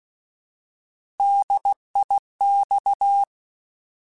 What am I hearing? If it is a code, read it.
Morse code, heard as a keyed tone, about 16 wpm, 800 Hz. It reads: DIX
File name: Alphabetical / Alphabetical mp3s / Ds @15